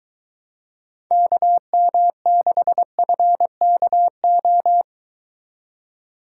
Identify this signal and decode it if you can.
Morse code, heard as a keyed tone, about 23 wpm, 700 Hz. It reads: KM6FKO